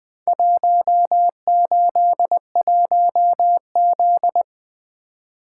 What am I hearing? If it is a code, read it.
Morse code, heard as a keyed tone, about 20 wpm, 700 Hz. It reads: 181Z
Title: Four Letter-Number Combinations - 20wpm